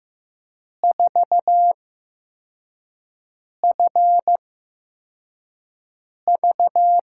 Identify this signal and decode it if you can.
Morse code, heard as a keyed tone, about 15 wpm, spaced as for 4 wpm, 700 Hz. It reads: 4FV
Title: USA Call Signs - 6 Characters - 8x Character Spacing - 15wpm